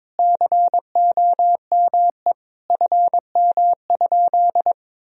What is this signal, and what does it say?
Morse code, heard as a keyed tone, about 22 wpm, 700 Hz. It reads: COME FM?
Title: Sentences from Top 200 Words and Abbreviations 22wpm